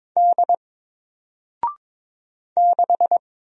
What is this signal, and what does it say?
Morse code, heard as a keyed tone, about 22 wpm, 700 Hz. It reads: D 6
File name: Single-Letter-Number-22wpm